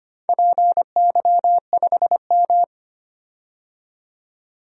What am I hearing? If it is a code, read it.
Morse code, heard as a keyed tone, about 25 wpm, 700 Hz. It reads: PY5M
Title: DXCC-100 Call Signs - 4 Characters - 25wpm